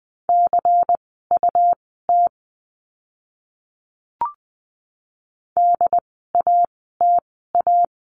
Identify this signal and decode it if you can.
Morse code, heard as a keyed tone, about 20 wpm, spaced as for 15 wpm, 700 Hz. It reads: CUT DATA
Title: 101 Lesson 17f - Letter u - Words 20wpm